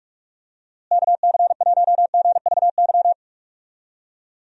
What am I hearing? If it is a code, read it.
Morse code, heard as a keyed tone, about 43 wpm, 700 Hz. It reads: KC1GVY
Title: USA Call Signs - 6 Characters - Repeated 2x - 45wpm